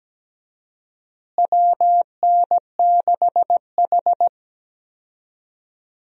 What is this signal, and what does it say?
Morse code, heard as a keyed tone, about 17 wpm, 700 Hz. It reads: WN6H
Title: USA Call Signs - 4 Characters - Repeated 3x - 17wpm